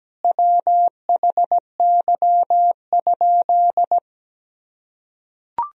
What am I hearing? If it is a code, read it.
Morse code, heard as a keyed tone, about 17 wpm, 700 Hz. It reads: WHY?
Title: Sentences from Top 200 Words and Abbreviations 17wpm